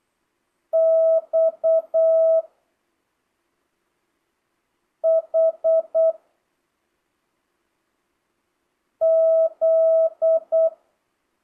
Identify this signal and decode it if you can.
Morse code, heard as a keyed tone, about 8 wpm, 650 Hz. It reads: X H Z
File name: jh8cbh / 2025cw